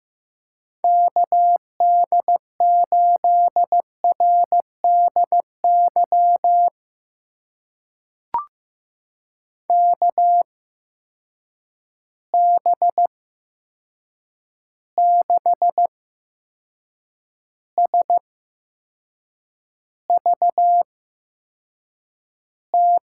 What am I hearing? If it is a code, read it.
Morse code, heard as a keyed tone, about 15 wpm, spaced as for 6 wpm, 700 Hz. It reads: KD8RDY KB6SVT
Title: USA Call Signs - 6 Characters - 8x Character Spacing - 15wpm